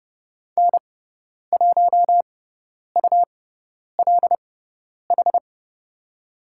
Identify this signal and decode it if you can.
Morse code, heard as a keyed tone, about 30 wpm, spaced as for 10 wpm, 700 Hz. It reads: N1ULH
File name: Lesson 17g - Letter u - Call Signs 30wpm